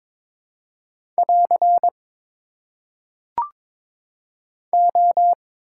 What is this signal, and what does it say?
Morse code, heard as a keyed tone, about 23 wpm, 700 Hz. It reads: <AR> O
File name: Top 90 QSO Elements - 4x Character Spacing - 22wpm